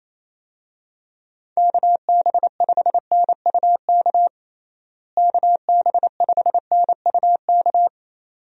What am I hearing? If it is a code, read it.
Morse code, heard as a keyed tone, about 28 wpm, 700 Hz. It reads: KB5NUK KB5NUK
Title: USA Call Signs - 6 Characters - Repeated 2x - 28wpm